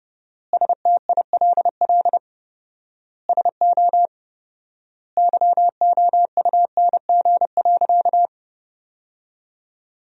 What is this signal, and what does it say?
Morse code, heard as a keyed tone, about 30 wpm, 700 Hz. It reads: STILL SO YOUNG.
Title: 4x Word-Spacing - Sentences from Top 400 Words - 30wpm